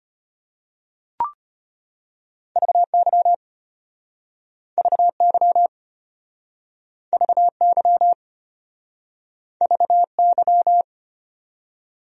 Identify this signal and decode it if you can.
Morse code, heard as a keyed tone, about 33 wpm, 700 Hz. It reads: VY VY VY VY